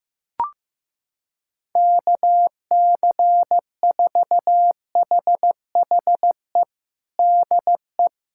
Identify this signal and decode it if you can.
Morse code, heard as a keyed tone, about 15 wpm, 700 Hz. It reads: KC4HHE DE
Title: USA Callsign DE Callsign - 6 Characters - 15wpm